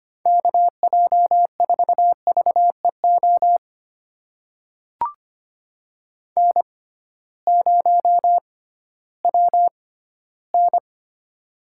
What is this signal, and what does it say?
Morse code, heard as a keyed tone, about 25 wpm, spaced as for 15 wpm, 700 Hz. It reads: KJ4VEO N0WN
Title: USA Call Signs - 6x Character Spacing - 25wpm